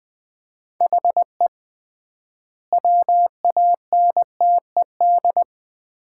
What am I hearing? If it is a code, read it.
Morse code, heard as a keyed tone, about 20 wpm, 700 Hz. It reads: HE WANTED
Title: 3x Word-Spacing - Sets of 3 Words - 20wpm